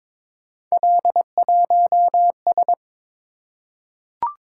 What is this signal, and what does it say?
Morse code, heard as a keyed tone, about 23 wpm, 700 Hz. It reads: L1S
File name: practice / 22wpm / Three Letter-Number Combinations - 22wpm